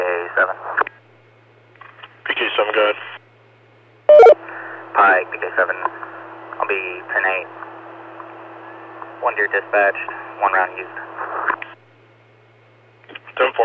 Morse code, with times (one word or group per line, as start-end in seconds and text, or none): none